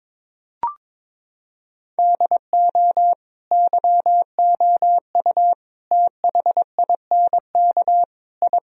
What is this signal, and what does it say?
Morse code, heard as a keyed tone, about 22 wpm, 700 Hz. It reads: DO YOU THINK I
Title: Sentences from Top 1000 Words - Review Part 1 - 22wpm